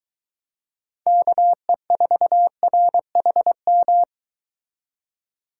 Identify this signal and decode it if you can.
Morse code, heard as a keyed tone, about 23 wpm, 700 Hz. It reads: KE4RHM